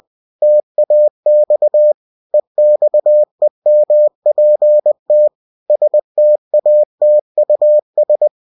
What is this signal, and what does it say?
Morse code, heard as a keyed tone, about 20 wpm, 600 Hz. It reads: TAX EXEMPT STATUS
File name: cw-books / winnie-the-pooh / winnie-20wpm-page-0121